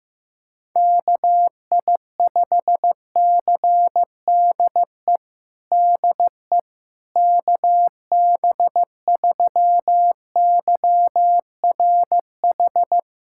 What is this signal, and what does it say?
Morse code, heard as a keyed tone, about 15 wpm, 700 Hz. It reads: KI5CDE DE KB3YRH
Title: USA Callsign DE Callsign 15wpm